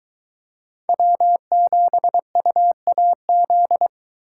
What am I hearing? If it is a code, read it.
Morse code, heard as a keyed tone, about 23 wpm, 700 Hz. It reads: W7UAZ